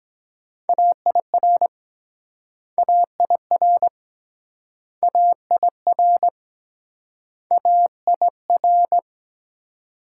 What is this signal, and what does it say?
Morse code, heard as a keyed tone, about 22 wpm, 700 Hz. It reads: AIR AIR AIR AIR